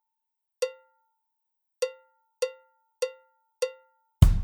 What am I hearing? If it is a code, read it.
There is no code.